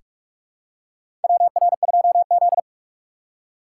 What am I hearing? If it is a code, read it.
Morse code, heard as a keyed tone, about 45 wpm, 700 Hz. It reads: WR2Z
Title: USA Call Signs - 4 Characters - 6x Character Spacing - 45wpm